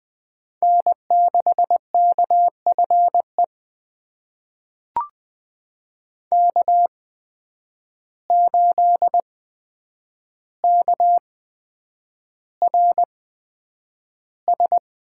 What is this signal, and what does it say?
Morse code, heard as a keyed tone, about 20 wpm, spaced as for 9 wpm, 700 Hz. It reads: N6KFE K8KRS
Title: USA Call Signs - 5 Characters - 8x Character Spacing - 20wpm